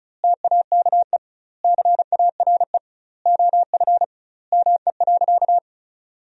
Morse code, heard as a keyed tone, about 36 wpm, 700 Hz. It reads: TAKE CARE OF ME.
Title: 2x Word-Spacing - Sentences from Top 300 Words - 35wpm